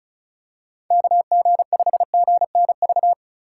Morse code, heard as a keyed tone, about 36 wpm, 700 Hz. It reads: KG5GNV
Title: USA Call Signs - 6 Characters - 3x Character Spacing - 35wpm